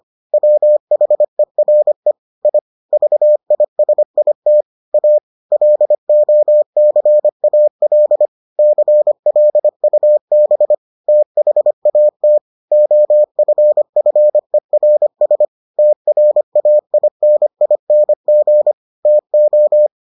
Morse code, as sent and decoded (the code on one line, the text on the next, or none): .-- .... .-. / .. / ...- .. ... .. - / .- / .-.. --- -.-. .- .-.. / -.-. .-.. ..- -... / - .... .- - / --- ..-. ..-. .-. ... / - .-. .- .. -. .. -. --. / - ---
WHERE I VISIT A LOCAL CLUB THAT OFFERS TRAINING TO